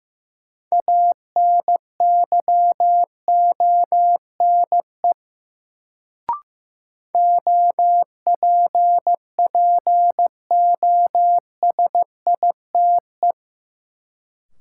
Morse code, heard as a keyed tone, about 15 wpm, 700 Hz. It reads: ANYONE OPPOSITE
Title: Sets of 5 Words Encore - New Word Drill - 15wpm